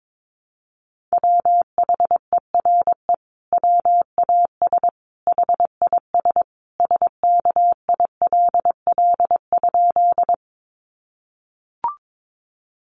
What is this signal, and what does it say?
Morse code, heard as a keyed tone, about 22 wpm, 700 Hz. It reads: WHERE WAS HIS SKILL?
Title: Sentences from Top 1200 Words 22wpm